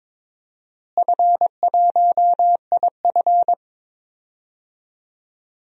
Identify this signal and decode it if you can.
Morse code, heard as a keyed tone, about 22 wpm, 700 Hz. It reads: F1IF
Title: DXCC-100 Call Signs - 4 Characters - Repeated 3x - 22wpm